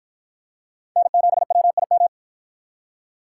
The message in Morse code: N7WIM

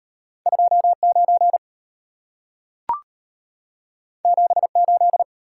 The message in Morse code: 29 78

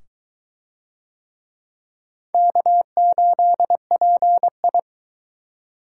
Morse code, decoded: K8PI

